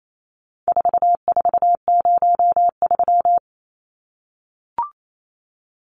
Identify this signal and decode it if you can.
Morse code, heard as a keyed tone, about 28 wpm, 700 Hz. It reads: 4403
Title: Numbers - 4-Digit 28wpm